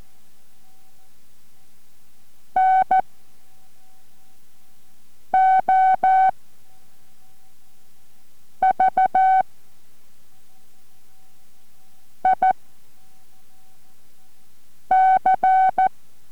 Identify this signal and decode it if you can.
Morse code, heard as a keyed tone, about 14 wpm, 750 Hz. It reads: N O V I C